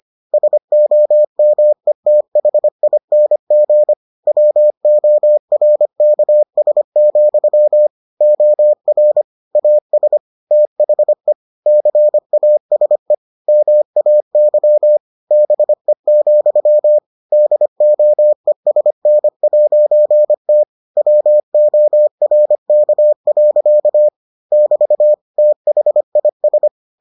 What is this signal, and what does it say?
Morse code, heard as a keyed tone, about 25 wpm, 600 Hz. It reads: SOMETHING WORKS, OR AS THE CASE MAY BE, DOESN'T WORK. <BT> THIS